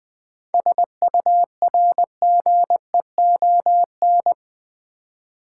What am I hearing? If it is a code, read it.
Morse code, heard as a keyed tone, about 20 wpm, 700 Hz. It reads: SURGEON
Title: 7 Letter Words - First 3 Letters Given - 20wpm